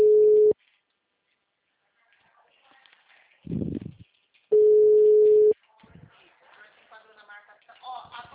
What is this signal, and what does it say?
Ringback tone (Europe, 425 Hz), with speech.
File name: out- -2005- - - .55020